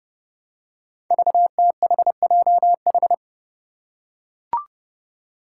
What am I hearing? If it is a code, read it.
Morse code, heard as a keyed tone, about 30 wpm, 700 Hz. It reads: VTHJH
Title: Five Challenging Letter-Number Combinations 30wpm